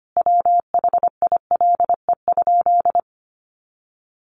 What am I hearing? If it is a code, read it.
Morse code, heard as a keyed tone, about 25 wpm, 700 Hz. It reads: WHILE?
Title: Sentences from Top 300 Words 25wpm